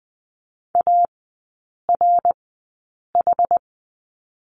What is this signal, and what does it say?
Morse code, heard as a keyed tone, about 20 wpm, spaced as for 8 wpm, 700 Hz. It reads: ARH
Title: 081 Lesson 14e - Number 2 - Three Characters 20wpm